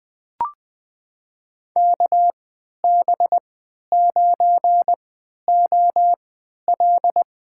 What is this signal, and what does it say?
Morse code, heard as a keyed tone, about 21 wpm, spaced as for 12 wpm, 700 Hz. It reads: KB9OL
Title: USA Call Signs - 3x Character Spacing - 20wpm